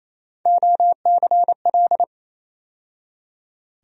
Morse code, heard as a keyed tone, about 28 wpm, 700 Hz. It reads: OCL
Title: USA Call Sign Postfixes - 3 Letters 28wpm